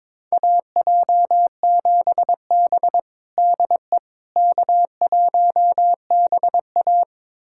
Morse code, heard as a keyed tone, about 22 wpm, 700 Hz. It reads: AJ7B DE K1BA
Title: USA Callsign DE Callsign - 4 Characters - 22wpm